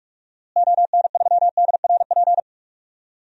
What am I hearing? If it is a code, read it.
Morse code, heard as a keyed tone, about 45 wpm, 700 Hz. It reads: ON3DRP
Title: DXCC-100 Call Signs - 6 Characters - Repeated 2x - 45wpm